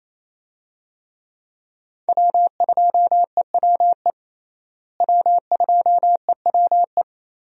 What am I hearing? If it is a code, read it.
Morse code, heard as a keyed tone, about 28 wpm, 700 Hz. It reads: W2EWE W2EWE
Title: USA Call Signs - 5 Characters - Repeated 3x - 28wpm